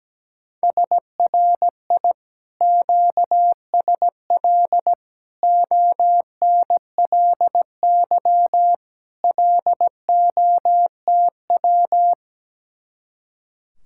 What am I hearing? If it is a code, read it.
Morse code, heard as a keyed tone, about 17 wpm, 700 Hz. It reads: SRI QSL ONLY LOTW
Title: Speed-Racing Common QSO Phrases Encore - 17wpm